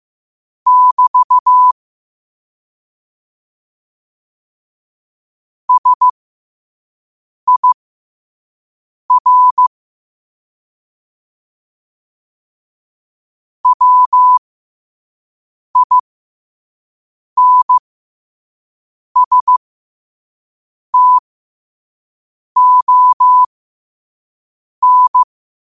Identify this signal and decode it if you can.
Morse code, heard as a keyed tone, about 15 wpm, spaced as for 5 wpm, 1000 Hz. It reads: <BT> SIR WINSTON
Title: 5 WPM morse code quotes for Fri, 01 Mar 2024